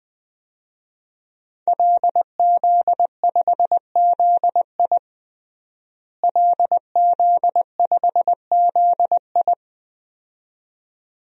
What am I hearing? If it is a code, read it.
Morse code, heard as a keyed tone, about 20 wpm, 700 Hz. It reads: LZ5ZI LZ5ZI
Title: DXCC-100 Call Signs - 5 Characters - Repeated 2x - 20wpm